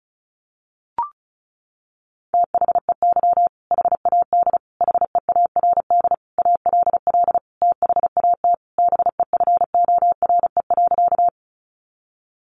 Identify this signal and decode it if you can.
Morse code, heard as a keyed tone, about 34 wpm, 700 Hz. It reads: THEY HAD HEARD ALL THAT BEFORE.
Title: Sentences from Top 400 Words 35wpm